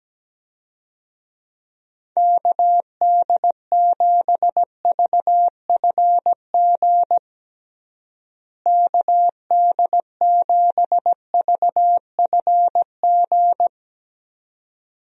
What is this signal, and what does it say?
Morse code, heard as a keyed tone, about 17 wpm, 700 Hz. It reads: KD7VFG KD7VFG
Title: USA Call Signs - 6 Characters - Repeated 2x - 17wpm